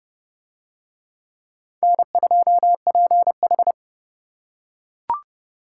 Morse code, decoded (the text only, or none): N2PH